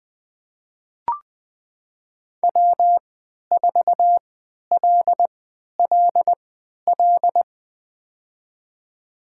Morse code, heard as a keyed tone, about 20 wpm, spaced as for 11 wpm, 700 Hz. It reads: W4LLL
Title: USA Call Signs - 5 Characters - 3x Character Spacing - 20wpm